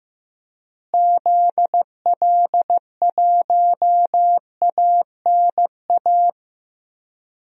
Morse code, heard as a keyed tone, about 15 wpm, 700 Hz. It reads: ZL1ANA
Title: DXCC-100 Call Signs - 6 Characters - Repeated 3x - 15wpm